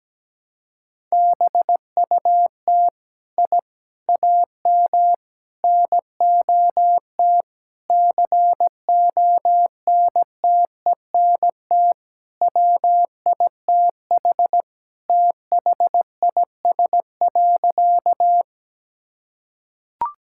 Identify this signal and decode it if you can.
Morse code, heard as a keyed tone, about 17 wpm, 700 Hz. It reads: BUT I AM NOT CONTENT WITH THIS.
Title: Sentences from Top 1000 Words - Review Part 6 - 17wpm